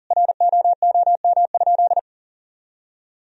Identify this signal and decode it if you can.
Morse code, heard as a keyed tone, about 40 wpm, 700 Hz. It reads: ROOM?
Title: Sentences from Top 1000 Words - Review Part 4 - 40wpm